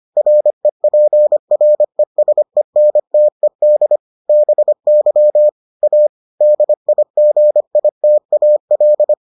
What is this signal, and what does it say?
Morse code, heard as a keyed tone, about 25 wpm, 600 Hz. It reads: REPRESENTED BY A DIGITAL